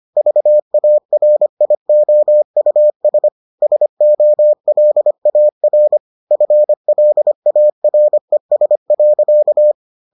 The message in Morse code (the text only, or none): VARIOUS SOLAR FLARES.